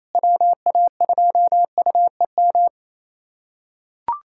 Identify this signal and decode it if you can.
Morse code, heard as a keyed tone, about 28 wpm, 700 Hz. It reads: WA2UEM
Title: USA Callsign DE Callsign - 6 Characters - 28wpm